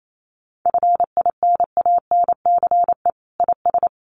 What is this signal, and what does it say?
Morse code, heard as a keyed tone, about 28 wpm, 700 Hz. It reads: FINANCE IS
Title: Sentences from Top 1500 Words 28wpm